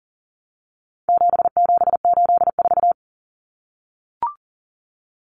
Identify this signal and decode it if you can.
Morse code, heard as a keyed tone, about 40 wpm, 700 Hz. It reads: 7784
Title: Common Cut Numbers - 4-Digit 40wpm